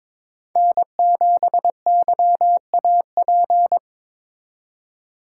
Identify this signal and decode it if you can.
Morse code, heard as a keyed tone, about 21 wpm, 700 Hz. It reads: N7YAP
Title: USA Call Signs - 5 Characters - 3x Character Spacing - 22wpm